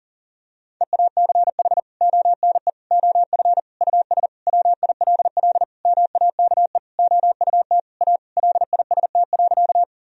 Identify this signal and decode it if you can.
Morse code, heard as a keyed tone, about 40 wpm, 700 Hz. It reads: EACH ONE OF US WILL MAKE OUT A LIST.
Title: Sentences from Top 1000 Words - Review Part 2 - 40wpm